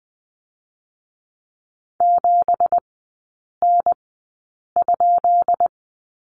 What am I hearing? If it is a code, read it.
Morse code, heard as a keyed tone, about 19 wpm, spaced as for 8 wpm, 690 Hz. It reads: 7N?
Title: 205 Lesson 33e - Letter j - Three Characters 20wpm